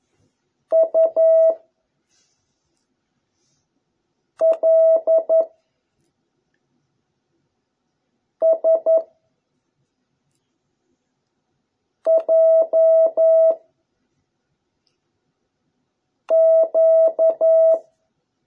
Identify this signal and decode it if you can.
Morse code, heard as a keyed tone, about 11 wpm, 660 Hz. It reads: U L S J Q